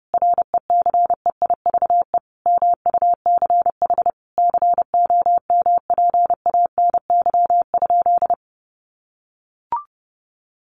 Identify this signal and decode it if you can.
Morse code, heard as a keyed tone, about 30 wpm, 700 Hz. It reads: RECEIVE MUCH COMPANY?
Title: Sentences from Top 1000 Words 30wpm